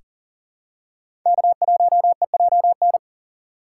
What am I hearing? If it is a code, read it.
Morse code, heard as a keyed tone, about 40 wpm, 700 Hz. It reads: K1EJN
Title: USA Call Signs - 5 Characters - 2x Character Spacing - 40wpm